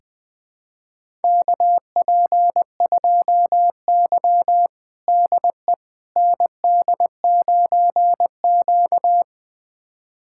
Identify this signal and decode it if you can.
Morse code, heard as a keyed tone, about 20 wpm, 700 Hz. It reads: KP2Y DE ND9Q
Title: USA Callsign DE Callsign - 4 Characters - 20wpm